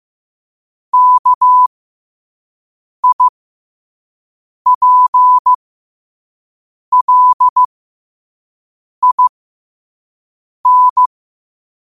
Morse code, heard as a keyed tone, about 15 wpm, spaced as for 5 wpm, 1000 Hz. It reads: KIPLIN
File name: qotdfiles / 5wpm